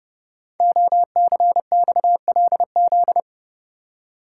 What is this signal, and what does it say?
Morse code, heard as a keyed tone, about 30 wpm, 700 Hz. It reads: OCXLZ
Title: Code Groups - 5 Letters - 30wpm